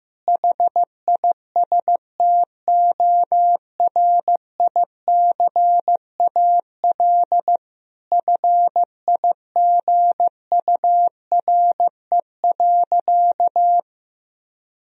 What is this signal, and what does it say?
Morse code, heard as a keyed tone, about 15 wpm, 700 Hz. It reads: HISTORICAL FIGURE.
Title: Sentences from Top 1000 Words - Review Part 4 - 15wpm